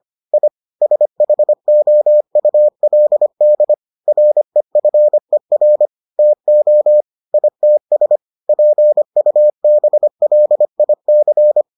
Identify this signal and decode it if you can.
Morse code, heard as a keyed tone, about 25 wpm, 600 Hz. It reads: I SHOULD REFER TO ITS PUBLIC